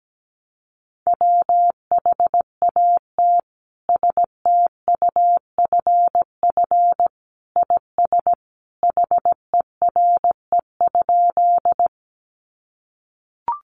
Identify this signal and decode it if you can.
Morse code, heard as a keyed tone, about 17 wpm, 700 Hz. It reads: WHAT STUFF IS HERE?
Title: Sentences from Top 1400 Words 17wpm